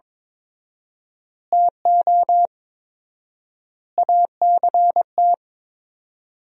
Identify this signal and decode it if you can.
Morse code, heard as a keyed tone, about 21 wpm, 700 Hz. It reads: TO ACT
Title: 4x Word-Spacing - Sentences from Top 500 Words - 22wpm